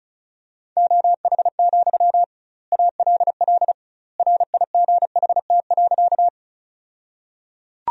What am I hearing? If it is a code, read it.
Morse code, heard as a keyed tone, about 34 wpm, 700 Hz. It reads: OH, ALL RIGHT.
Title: 2x Word-Spacing - Sentences from Top 500 Words - 35wpm